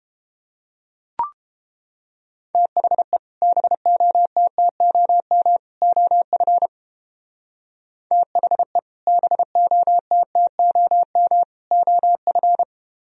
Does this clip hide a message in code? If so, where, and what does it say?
Yes: Morse code from 2.5 to 12.6 s, heard as a keyed tone, about 30 wpm, 700 Hz. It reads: THE BOTTOM OF THE BOTTOM OF